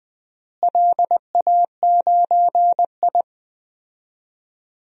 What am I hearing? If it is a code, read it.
Morse code, heard as a keyed tone, about 20 wpm, 700 Hz. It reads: LA9I
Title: DXCC-100 Call Signs - 4 Characters - Repeated 3x - 20wpm